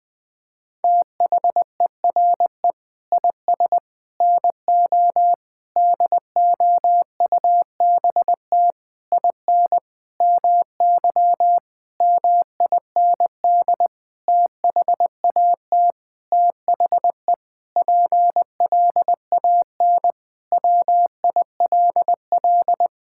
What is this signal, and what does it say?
Morse code, heard as a keyed tone, about 20 wpm, 700 Hz. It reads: THERE IS NO DOUBT IN MY MIND THAT THE PLAN WILL
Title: Sentences with 9-Word Phrases - Spoken Beginning - 20wpm